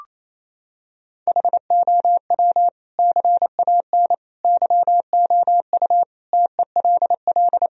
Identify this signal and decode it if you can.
Morse code, heard as a keyed tone, about 28 wpm, 700 Hz. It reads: HOW CAN YOU TELL
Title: Sentences from Top 1000 Words - Review Part 3 - 28wpm